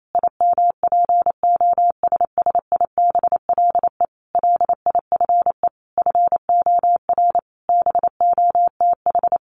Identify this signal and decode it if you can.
Morse code, heard as a keyed tone, about 28 wpm, 700 Hz. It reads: IMPOSSIBLE LIFE FOR BOTH
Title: Sentences from Top 1100 Words 28wpm